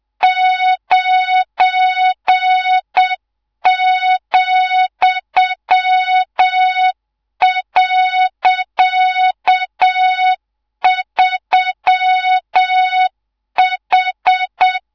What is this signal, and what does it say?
Morse code, heard as a keyed tone, about 7 wpm, 3700 Hz, with noise. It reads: 9,.3H
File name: Morze-HA2MN / hangp / 2 03 GYAKORLAT